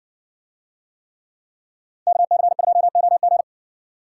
Morse code, heard as a keyed tone, about 55 wpm, 700 Hz. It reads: KC2OG